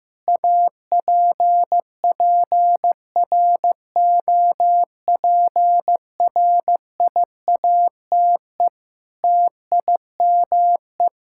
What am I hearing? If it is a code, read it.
Morse code, heard as a keyed tone, about 15 wpm, 700 Hz. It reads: APPROPRIATE TIME
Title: Sentences from Top 1000 Words - Review Part 5 - 15wpm